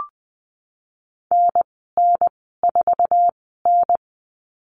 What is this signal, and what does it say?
Morse code, heard as a keyed tone, about 20 wpm, spaced as for 15 wpm, 700 Hz. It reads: NN4N
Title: 055 Lesson 10g - Letter r - Call Signs 20wpm